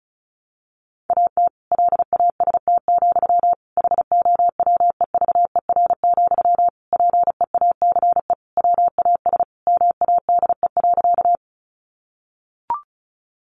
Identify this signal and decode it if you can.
Morse code, heard as a keyed tone, about 34 wpm, 700 Hz. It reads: AT LAST, HOWEVER, PEACE WAS MADE.
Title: Sentences from Top 1500 Words 35wpm